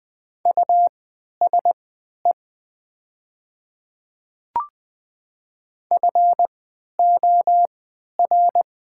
Morse code, heard as a keyed tone, about 20 wpm, spaced as for 11 wpm, 700 Hz. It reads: USE FOR
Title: 3x Character Spacing - Top 100 Words - 20wpm